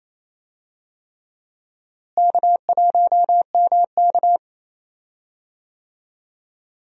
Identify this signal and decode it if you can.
Morse code, heard as a keyed tone, about 28 wpm, 700 Hz. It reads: K1MK
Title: USA Call Signs - 4 Characters - Repeated 3x - 28wpm